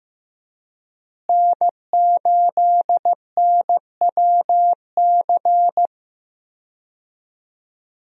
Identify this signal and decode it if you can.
Morse code, heard as a keyed tone, about 16 wpm, 700 Hz. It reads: N8NWC
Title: USA Call Signs - 5 Characters - Repeated 3x - 15wpm